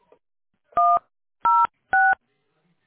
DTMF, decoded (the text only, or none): 106